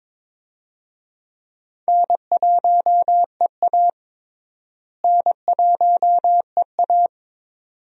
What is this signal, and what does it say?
Morse code, heard as a keyed tone, about 22 wpm, 700 Hz. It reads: N1EA N1EA
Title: USA Call Signs - 4 Characters - Repeated 2x - 22wpm